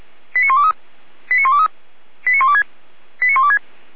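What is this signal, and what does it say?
Five-tone selective call (ZVEI), heard as quick runs of five tones, tones of about 80 ms. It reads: 89123 89123 89126 89126
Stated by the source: Received on an SDR, ZVEI five-tone call 89770.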